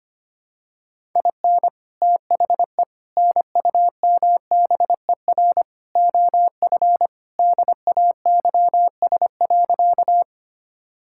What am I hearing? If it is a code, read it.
Morse code, heard as a keyed tone, about 25 wpm, 700 Hz. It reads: IN THE NUMBER OF DAYS.
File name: practice / 25wpm / Sentences with 4-Word Phrases - 25wpm